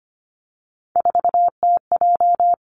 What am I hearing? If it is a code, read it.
Morse code, heard as a keyed tone, about 25 wpm, 700 Hz. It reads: 4TJ
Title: Three Challenging Letter-Number Combinations 25wpm